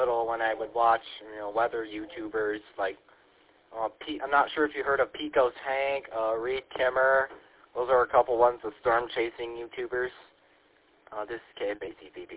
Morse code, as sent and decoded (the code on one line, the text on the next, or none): none